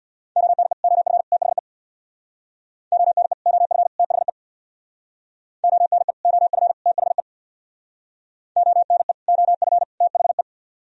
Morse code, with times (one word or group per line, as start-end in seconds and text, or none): 0.4-0.7 ONE
0.8-1.2 OF
1.3-1.6 THE
2.9-3.3 ONE
3.5-3.9 OF
4.0-4.3 THE
5.6-6.1 ONE
6.3-6.7 OF
6.9-7.2 THE
8.6-9.1 ONE
9.3-9.8 OF
10.0-10.4 THE